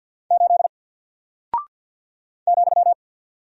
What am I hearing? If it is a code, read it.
Morse code, heard as a keyed tone, about 49 wpm, 700 Hz. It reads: 8 ,
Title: Single Character (Letter:Number:Punctuation:Prosign) Alt 50wpm